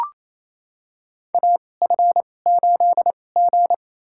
Morse code, heard as a keyed tone, about 28 wpm, spaced as for 20 wpm, 700 Hz. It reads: AF8G
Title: USA Call Signs - 4 Characters - 2x Character Spacing - 28wpm